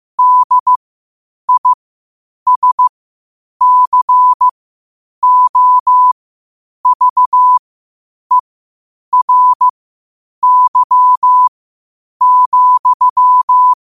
Morse code, heard as a keyed tone, about 15 wpm, spaced as for 8 wpm, 1000 Hz. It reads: DISCOVERY,
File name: qotdfiles / 8wpm